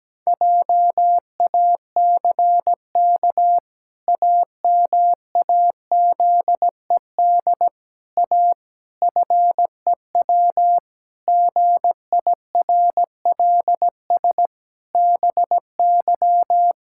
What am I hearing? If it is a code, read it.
Morse code, heard as a keyed tone, about 17 wpm, 700 Hz. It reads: JACK AMAZED A FEW GIRLS BY
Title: Pangram Challenge v2 - 17wpm